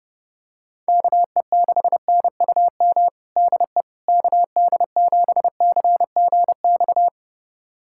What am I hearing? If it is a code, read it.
Morse code, heard as a keyed tone, about 32 wpm, 700 Hz. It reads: KE6NUM DE KD7CGX
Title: USA Callsign DE Callsign - 6 Characters - 30wpm